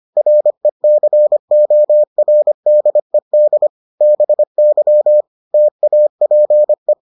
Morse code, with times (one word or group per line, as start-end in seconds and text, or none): 0.2-3.7 RECORDED
4.0-5.2 BY
5.5-6.9 TAPE